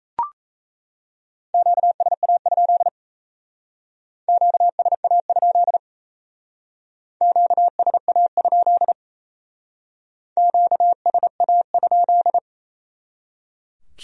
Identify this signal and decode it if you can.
Morse code, heard as a keyed tone, about 34 wpm, 700 Hz. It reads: QSA? QSA? QSA? QSA?
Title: Speed-Racing Q-Codes - 28wpm